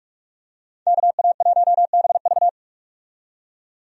Morse code, heard as a keyed tone, about 46 wpm, 700 Hz. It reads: KA1BV
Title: USA Call Signs - 5 Characters - 45wpm